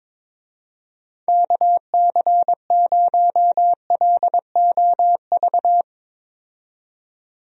Morse code, heard as a keyed tone, about 22 wpm, 700 Hz. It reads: KC0LOV
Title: USA Call Signs - 6 Characters - 22wpm